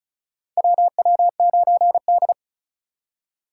Morse code, heard as a keyed tone, about 36 wpm, 700 Hz. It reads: WW9D
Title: USA Call Signs - 3x Character Spacing - 35wpm